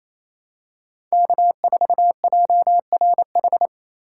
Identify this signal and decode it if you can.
Morse code, heard as a keyed tone, about 28 wpm, 700 Hz. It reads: K4JRH
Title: USA Call Signs - 5 Characters - 4x Character Spacing - 28wpm